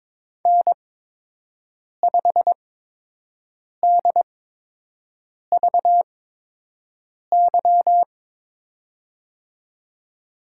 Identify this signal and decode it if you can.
Morse code, heard as a keyed tone, about 21 wpm, spaced as for 6 wpm, 700 Hz. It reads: N5DVY
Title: USA Call Signs - 5 Characters - 8x Character Spacing - 22wpm